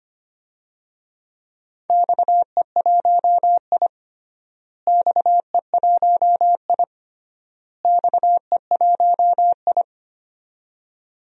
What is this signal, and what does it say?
Morse code, heard as a keyed tone, about 25 wpm, 700 Hz. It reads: XE1I XE1I XE1I
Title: DXCC-100 Call Signs - 4 Characters - Repeated 3x - 25wpm